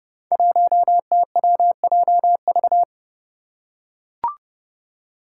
Morse code, heard as a keyed tone, about 31 wpm, 700 Hz. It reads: 1TWJV
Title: Five Challenging Letter-Number Combinations 30wpm